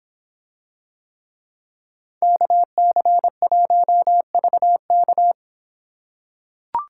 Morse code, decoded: KC1VK

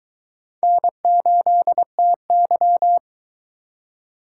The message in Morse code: N8TY

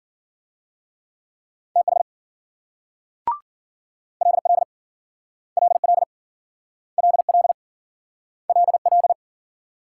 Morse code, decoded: TH LL LL LL LL

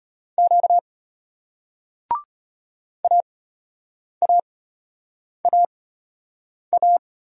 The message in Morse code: Q AAAA

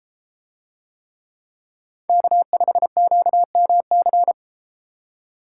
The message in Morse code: K5QMC

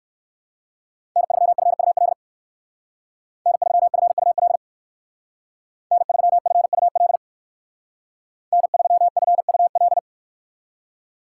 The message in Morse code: N3FUL N3FUL N3FUL N3FUL

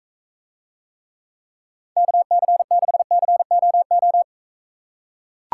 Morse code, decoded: KC6COO

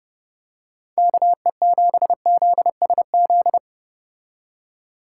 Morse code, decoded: KE7ZSZ

